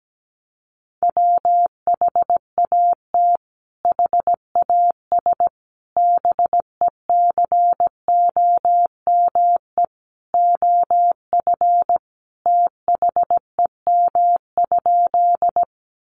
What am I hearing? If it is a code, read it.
Morse code, heard as a keyed tone, about 17 wpm, 700 Hz. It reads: WHAT HAS BECOME OF THEM?
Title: Sentences from Top 600 Words 17wpm